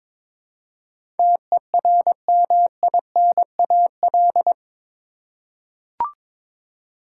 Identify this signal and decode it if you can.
Morse code, heard as a keyed tone, about 22 wpm, 700 Hz. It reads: TERMINAL
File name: practice / 22wpm / Eight Letter Words - 22wpm